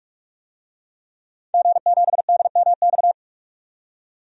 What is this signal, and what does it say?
Morse code, heard as a keyed tone, about 45 wpm, 700 Hz. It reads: G7DMX
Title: DXCC-100 Call Signs - 5 Characters - 45wpm